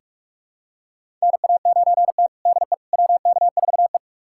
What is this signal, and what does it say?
Morse code, heard as a keyed tone, about 45 wpm, 700 Hz. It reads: NA9T DE WK4E
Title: USA Callsign DE Callsign - 4 Characters - 45wpm